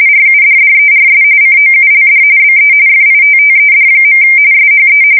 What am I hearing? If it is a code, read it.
RTTY, about 45.45 baud. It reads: RUDDERS COULD BE FITTED. SHE
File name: fsk170 / battleshipnj